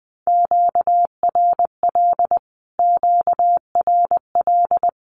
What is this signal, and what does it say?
Morse code, heard as a keyed tone, about 20 wpm, 700 Hz. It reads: QRL QRL
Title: Q-Codes Repeated 3x 20wpm